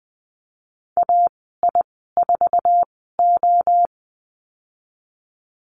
Morse code, heard as a keyed tone, about 20 wpm, spaced as for 14 wpm, 700 Hz. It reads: AI4O
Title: 082 Lesson 14g - Number 2 - Call Signs 20wpm